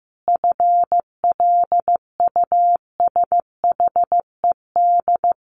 Morse code, heard as a keyed tone, about 15 wpm, 700 Hz. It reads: FLUSHED
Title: A Sound Of Thunder 15wpm